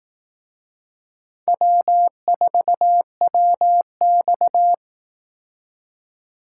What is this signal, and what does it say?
Morse code, heard as a keyed tone, about 18 wpm, 700 Hz. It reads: W4WX
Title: Speed Racing - USA 1x2 Call Signs 15wpm